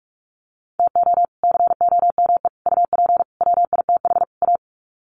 Morse code, heard as a keyed tone, about 44 wpm, 700 Hz. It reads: TO COME UP WITH A
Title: Sets of 5 Words 45wpm